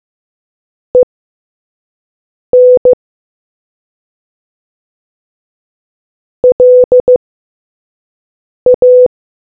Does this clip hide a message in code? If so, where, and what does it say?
Yes: Morse code from 0.9 to 9.1 s, heard as a keyed tone, about 15 wpm, spaced as for 5 wpm, 500 Hz. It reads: EN LA